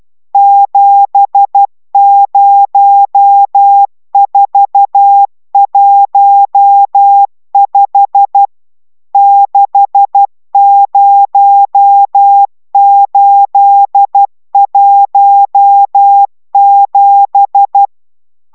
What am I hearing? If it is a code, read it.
Morse code, heard as a keyed tone, about 12 wpm, 800 Hz. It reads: 70415 60817